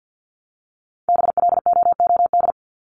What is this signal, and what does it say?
Morse code, heard as a keyed tone, about 50 wpm, 700 Hz. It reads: BLOOD
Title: Top 1001-1250 Words 50wpm